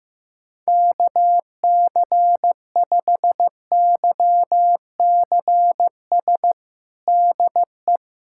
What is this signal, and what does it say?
Morse code, heard as a keyed tone, about 15 wpm, 700 Hz. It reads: KC5YCS DE
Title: USA Callsign DE Callsign - 6 Characters - 15wpm